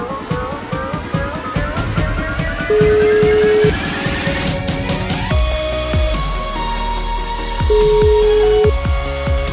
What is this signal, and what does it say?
Ringback tone (Europe, 425 Hz), with noise.